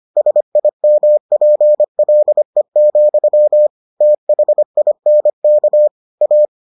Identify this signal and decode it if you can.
Morse code, heard as a keyed tone, about 25 wpm, 600 Hz. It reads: SIMPLE, THINK A